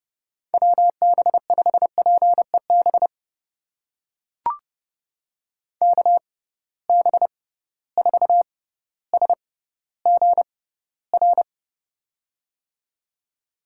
Morse code, heard as a keyed tone, about 30 wpm, spaced as for 15 wpm, 700 Hz. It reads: WB5PEB KB4SGR
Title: USA Call Signs - 6 Characters - 6x Character Spacing - 30wpm